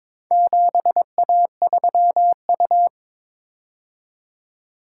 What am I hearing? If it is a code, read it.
Morse code, heard as a keyed tone, about 22 wpm, 700 Hz. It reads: 7A3U
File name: practice / 22wpm / Four Letter-Number Combinations - 22wpm